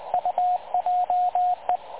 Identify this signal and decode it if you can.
Morse code, heard as a keyed tone, about 21 wpm, 720 Hz, with radio noise. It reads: UJE
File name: dlazaawansowanych / nagraniazpasma / websdr recording 3565.0kHz